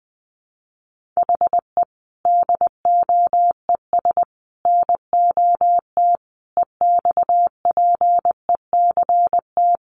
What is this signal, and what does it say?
Morse code, heard as a keyed tone, about 19 wpm, 700 Hz. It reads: HE DOES NOT EXPECT
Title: Sentences from Top 1100 Words 20wpm